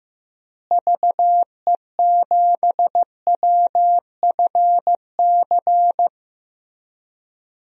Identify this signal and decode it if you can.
Morse code, heard as a keyed tone, about 15 wpm, 700 Hz. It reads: VE7WFC
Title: DXCC-100 Call Signs - 6 Characters - Repeated 3x - 15wpm